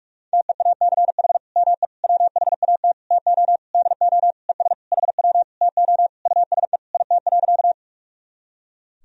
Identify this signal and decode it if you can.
Morse code, heard as a keyed tone, about 41 wpm, 700 Hz. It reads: TEACH ME WHAT TO DO ES HW TO USE IT.
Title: Sentences from Top 500 Words and Abbreviations 45wpm